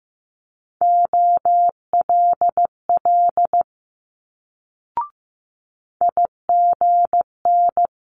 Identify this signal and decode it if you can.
Morse code, heard as a keyed tone, about 15 wpm, 700 Hz. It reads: OLL IGN
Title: Multi-letter Phonemes Quick Fire 15wpm